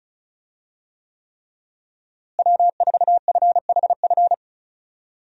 Morse code, heard as a keyed tone, about 34 wpm, 700 Hz. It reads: W4FHF
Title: USA Call Signs - 5 Characters - Repeated 3x - 35wpm